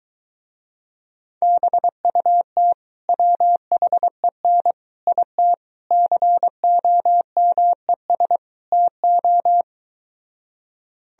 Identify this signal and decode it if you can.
Morse code, heard as a keyed tone, about 23 wpm, 700 Hz. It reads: BUT WHEN IT COMES TO